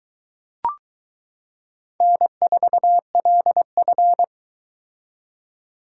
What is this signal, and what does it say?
Morse code, heard as a keyed tone, about 22 wpm, 700 Hz. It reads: N4LF